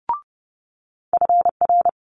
Morse code, heard as a keyed tone, about 30 wpm, 700 Hz. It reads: FR
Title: Multi-letter Phonemes Quick Fire 30wpm